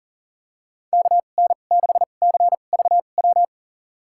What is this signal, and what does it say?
Morse code, heard as a keyed tone, about 40 wpm, spaced as for 29 wpm, 700 Hz. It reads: KN6CVW
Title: USA Call Signs - 6 Characters - 2x Character Spacing - 40wpm